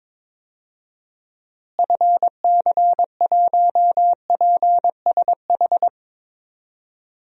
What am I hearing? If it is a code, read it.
Morse code, heard as a keyed tone, about 22 wpm, 700 Hz. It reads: FC1PSH
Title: DXCC-100 Call Signs - 6 Characters - Repeated 2x - 22wpm